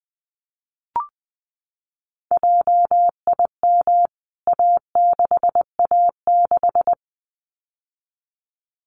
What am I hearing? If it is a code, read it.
Morse code, heard as a keyed tone, about 20 wpm, 700 Hz. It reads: JIM A6A6
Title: CWT Exchange 20wpm